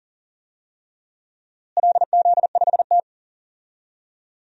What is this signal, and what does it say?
Morse code, heard as a keyed tone, about 40 wpm, 700 Hz. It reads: LZ5T
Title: DXCC-100 Call Signs - 4 Characters - Repeated 2x - 40wpm